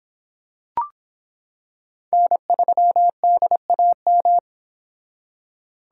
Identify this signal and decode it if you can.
Morse code, heard as a keyed tone, about 24 wpm, 700 Hz. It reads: N3DAM